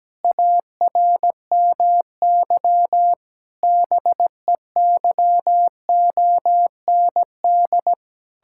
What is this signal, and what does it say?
Morse code, heard as a keyed tone, about 17 wpm, 700 Hz. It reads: ARMY BEYOND
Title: Sentences from Top 1000 Words - Review Part 6 - 17wpm